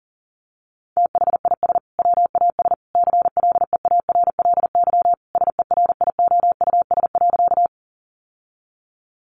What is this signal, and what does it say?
Morse code, heard as a keyed tone, about 40 wpm, 700 Hz. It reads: THIS WAS CLEARLY SERIOUS.
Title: Sentences from Top 1300 Words 40wpm